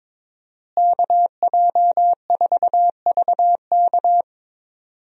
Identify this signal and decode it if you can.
Morse code, heard as a keyed tone, about 22 wpm, 700 Hz. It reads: KJ4VK
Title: USA Call Signs - 5 Characters - Repeated 3x - 22wpm